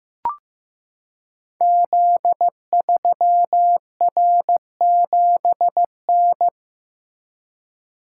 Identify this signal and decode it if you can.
Morse code, heard as a keyed tone, about 15 wpm, 700 Hz. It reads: Z3R7N
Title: Code Groups - 5 Letter:Numbers - 15wpm